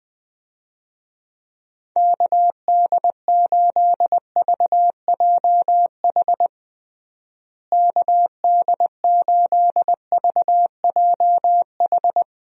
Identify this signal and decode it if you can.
Morse code, heard as a keyed tone, about 21 wpm, 700 Hz. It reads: KD8VJH KD8VJH
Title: USA Call Signs - 6 Characters - Repeated 2x - 20wpm